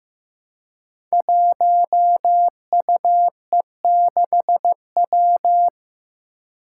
Morse code, heard as a keyed tone, about 15 wpm, 700 Hz. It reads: 1UE6W
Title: Five Challenging Letter-Number Combinations 15wpm